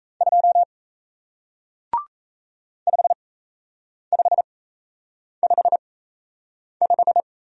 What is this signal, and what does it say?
Morse code, heard as a keyed tone, about 37 wpm, spaced as for 8 wpm, 700 Hz. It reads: 2 5555